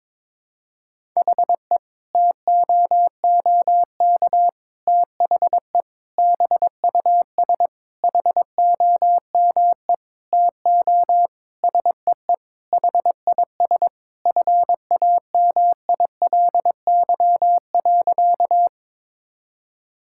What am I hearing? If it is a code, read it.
Morse code, heard as a keyed tone, about 22 wpm, 700 Hz. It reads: HE TOOK THE BUS HOME TO SEE HIS FAMILY.
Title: Sentences from Top 1000 Words - Review Part 6 - 22wpm